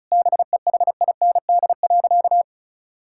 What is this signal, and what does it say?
Morse code, heard as a keyed tone, about 34 wpm, 700 Hz. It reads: BEHIND.
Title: Sentences from Top 1000 Words - Review Part 5 - 35wpm